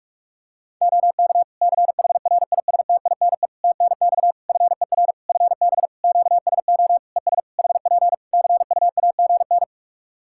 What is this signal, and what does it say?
Morse code, heard as a keyed tone, about 46 wpm, 700 Hz. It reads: OK CHRISTINE TNX FER FB QSO ES HP CUAGN
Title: Practice QSO - Exchange 4 - Ending - 45wpm